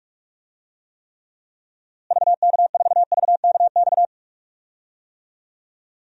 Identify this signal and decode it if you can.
Morse code, heard as a keyed tone, about 46 wpm, 700 Hz. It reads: VK4VKX
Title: DXCC-100 Call Signs - 6 Characters - Repeated 2x - 45wpm